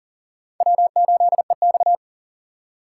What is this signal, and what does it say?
Morse code, heard as a keyed tone, about 40 wpm, 700 Hz. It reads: W8EX
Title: USA Call Signs - 6x Character Spacing - 40wpm